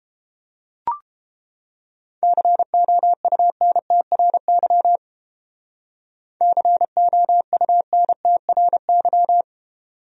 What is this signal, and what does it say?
Morse code, heard as a keyed tone, about 31 wpm, 700 Hz. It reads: COUNTRY COUNTRY